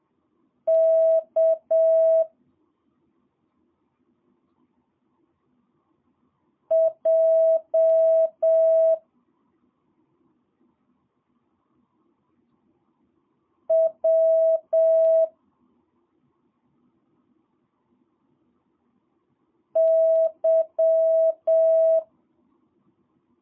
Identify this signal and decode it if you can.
Morse code, heard as a keyed tone, about 7 wpm, 650 Hz. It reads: K J W Y
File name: jh8cbh / 2022cw